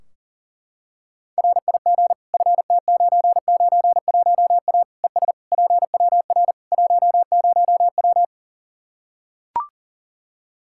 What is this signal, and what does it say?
Morse code, heard as a keyed tone, about 37 wpm, 700 Hz. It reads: RIG FT991A ES PWR 10W
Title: Practice QSO - Exchange 2 - Info - 40wpm